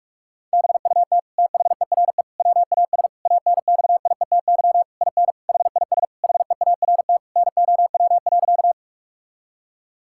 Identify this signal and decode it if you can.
Morse code, heard as a keyed tone, about 41 wpm, 700 Hz. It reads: BUT THERE WAS ANXIETY IN HIS HEART NOW.
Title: Sentences from Top 1600 Words 45wpm